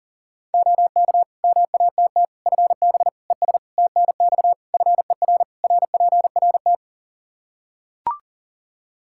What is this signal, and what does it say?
Morse code, heard as a keyed tone, about 40 wpm, 700 Hz. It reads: OK MATT FB ES TNX FER RPRT
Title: Practice QSO - Exchange 2 - Info - 40wpm